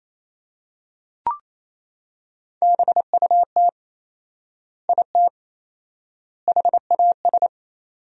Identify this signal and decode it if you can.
Morse code, heard as a keyed tone, about 28 wpm, 700 Hz. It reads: BUT IT HAS
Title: 4x Word-Spacing - Sentences from Top 100 Words - 28wpm